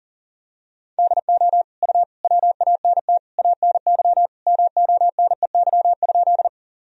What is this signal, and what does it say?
Morse code, heard as a keyed tone, about 40 wpm, 700 Hz. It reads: DO U WANT ANY MONEY?
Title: Sentences from Top 400 Words and Abbreviations 40wpm